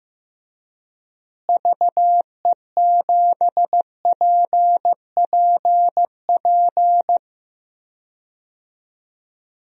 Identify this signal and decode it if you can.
Morse code, heard as a keyed tone, about 15 wpm, 700 Hz. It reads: VE7PPP
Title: DXCC-100 Call Signs - 6 Characters - Repeated 3x - 15wpm